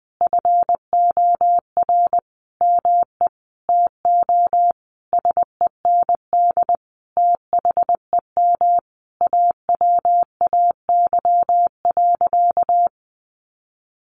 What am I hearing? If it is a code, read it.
Morse code, heard as a keyed tone, about 20 wpm, 700 Hz. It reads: FOR ME TO SEND THEM AWAY.
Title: Sentences from Top 1500 Words 20wpm